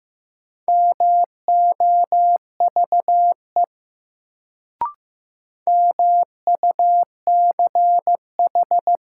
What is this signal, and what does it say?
Morse code, heard as a keyed tone, about 15 wpm, 700 Hz. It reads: MOVE MUCH
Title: Sentences from Top 200 Words - New Word Drill - 15wpm